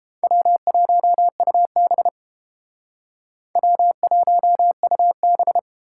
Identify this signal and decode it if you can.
Morse code, heard as a keyed tone, about 30 wpm, 700 Hz. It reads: W1UB W1UB